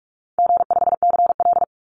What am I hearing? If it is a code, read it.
Morse code, heard as a keyed tone, about 44 wpm, 700 Hz. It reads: G5CL
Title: Easy Call Signs Repeated 3x - 45wpm